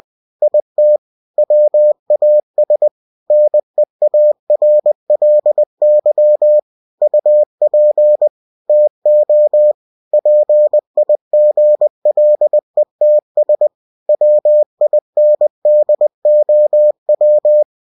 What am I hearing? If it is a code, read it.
Morse code, heard as a keyed tone, about 20 wpm, 600 Hz. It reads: IT WAS NEARLY UP TO PIGLETS WINDOW
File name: cw-books / winnie-the-pooh / winnie-20wpm-page-0091